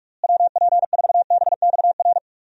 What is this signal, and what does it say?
Morse code, heard as a keyed tone, about 46 wpm, 700 Hz. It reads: WP4BXR
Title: USA Call Signs - 6 Characters - Repeated 2x - 45wpm